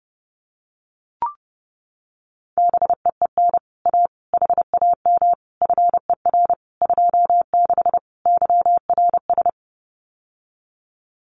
Morse code, heard as a keyed tone, about 32 wpm, 700 Hz. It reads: BEEN A HAM FER 26 YRS
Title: Common QSO Phrases Encore 30wpm